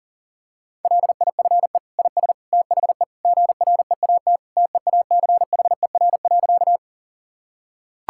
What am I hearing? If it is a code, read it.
Morse code, heard as a keyed tone, about 40 wpm, 700 Hz. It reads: LIFE IS THE GREAT TEACHER.
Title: Sentences from Top 1000 Words - Review Part 6 - 40wpm